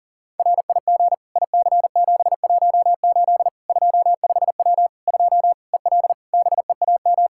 Morse code, heard as a keyed tone, about 40 wpm, 700 Hz. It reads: RIG IC718 25W 2 EL BEAM
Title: Practice QSO - Exchange 2 - Info - 40wpm